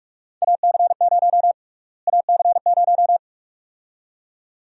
Morse code, heard as a keyed tone, about 46 wpm, 700 Hz. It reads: AC0 AC0
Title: USA Call Sign Prefixes - Repeated 2x - 45wpm